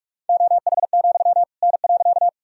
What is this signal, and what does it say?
Morse code, heard as a keyed tone, about 46 wpm, 700 Hz. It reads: OH, N.
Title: Sentences from Top 500 Words and Abbreviations 45wpm